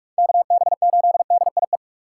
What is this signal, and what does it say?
Morse code, heard as a keyed tone, about 44 wpm, 700 Hz. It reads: KB8DIE
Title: USA Call Signs - 6 Characters - 6x Character Spacing - 45wpm